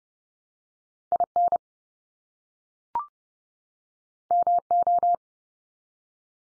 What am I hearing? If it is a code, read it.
Morse code, heard as a keyed tone, about 30 wpm, 700 Hz. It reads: IN MO
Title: US States 30WPM